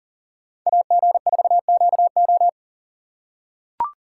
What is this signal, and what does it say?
Morse code, heard as a keyed tone, about 40 wpm, 700 Hz. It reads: AG4QO
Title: USA Call Signs - 5 Characters - 4x Character Spacing - 40wpm